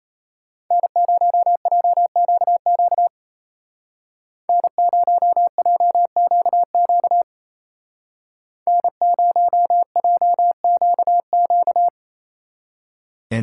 Morse code, heard as a keyed tone, about 31 wpm, 700 Hz. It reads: N0JQQ N0JQQ N0JQQ
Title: Speed Racing - USA 1x3 Call Signs 28wpm